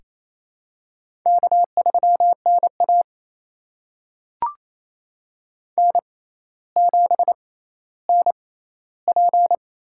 Morse code, heard as a keyed tone, about 27 wpm, spaced as for 14 wpm, 700 Hz. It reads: K3NA N7NP
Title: USA Call Signs - 6x Character Spacing - 28wpm